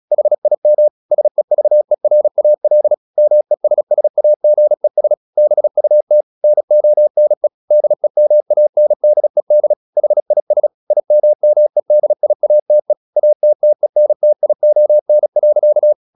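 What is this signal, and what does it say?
Morse code, heard as a keyed tone, about 36 wpm, 600 Hz. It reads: HIM SEVERAL MESSAGES BUT NONE DEMANDED HIS IMMEDIATE ATTENTION.
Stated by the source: Morse practice at 36 wpm, text of Ken Follett's novel Never, from CW med Gnister.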